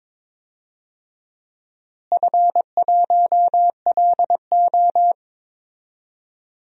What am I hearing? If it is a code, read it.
Morse code, heard as a keyed tone, about 22 wpm, 700 Hz. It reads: F1LO
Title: DXCC-100 Call Signs - 4 Characters - Repeated 2x - 22wpm